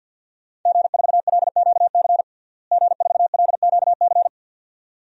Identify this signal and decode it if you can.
Morse code, heard as a keyed tone, about 46 wpm, 700 Hz. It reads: G4LQC G4LQC
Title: DXCC-100 Call Signs - 5 Characters - Repeated 2x - 50wpm